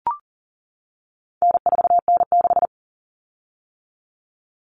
Morse code, heard as a keyed tone, about 40 wpm, 700 Hz. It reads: N4N6
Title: Common Cut Numbers - 4-Digit 40wpm